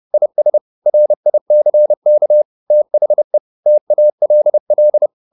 Morse code, heard as a keyed tone, about 30 wpm, 600 Hz. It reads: IS RICK THE TALL